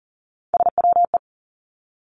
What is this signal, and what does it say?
Morse code, heard as a keyed tone, about 38 wpm, 700 Hz. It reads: SWE
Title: Three Challenging Letter-Number Combinations 40wpm